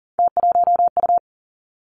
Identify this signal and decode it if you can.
Morse code, heard as a keyed tone, about 38 wpm, 700 Hz. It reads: T1U
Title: Three Challenging Letter-Number Combinations 40wpm